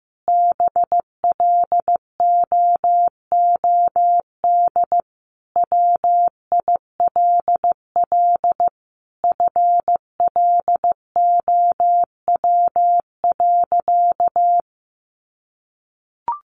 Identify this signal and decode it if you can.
Morse code, heard as a keyed tone, about 15 wpm, 700 Hz. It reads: BLOOD WILL FLOW.
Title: Sentences from Top 1200 Words 15wpm